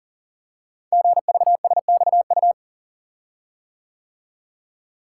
Morse code, heard as a keyed tone, about 38 wpm, 700 Hz. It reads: GVSXU
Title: Code Groups - 5 Letters - 40wpm